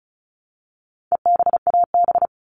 Morse code, heard as a keyed tone, about 34 wpm, 700 Hz. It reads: EBAB
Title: All Cut Numbers - 4-Digit 35wpm